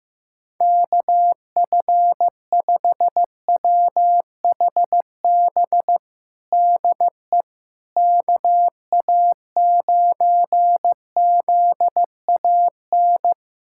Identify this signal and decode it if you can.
Morse code, heard as a keyed tone, about 15 wpm, 700 Hz. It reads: KF5WHB DE KA9ZAN
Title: USA Callsign DE Callsign - 6 Characters - 15wpm